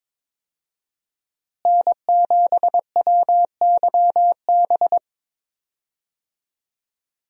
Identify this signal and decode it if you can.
Morse code, heard as a keyed tone, about 22 wpm, 700 Hz. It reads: N7WYB
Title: USA Call Signs - 5 Characters - Repeated 3x - 22wpm